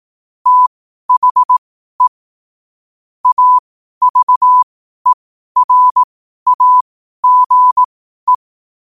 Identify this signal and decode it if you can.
Morse code, heard as a keyed tone, about 18 wpm, spaced as for 12 wpm, 1000 Hz. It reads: THE AVERAGE
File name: qotdfiles / 12wpm